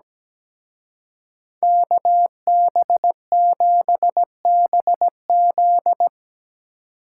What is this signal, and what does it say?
Morse code, heard as a keyed tone, about 19 wpm, 700 Hz. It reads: KB7BZ
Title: Speed Racing - USA 2x2 Call Signs 17wpm